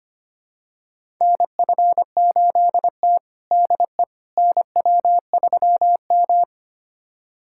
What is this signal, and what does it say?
Morse code, heard as a keyed tone, about 25 wpm, 700 Hz. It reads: NF8T DE NW3M
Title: USA Callsign DE Callsign - 4 Characters - 25wpm